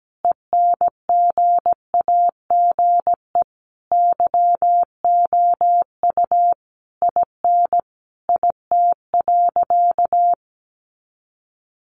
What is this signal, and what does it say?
Morse code, heard as a keyed tone, about 17 wpm, 700 Hz. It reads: ENGAGE YOU IN IT.
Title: Sentences from Top 1500 Words 17wpm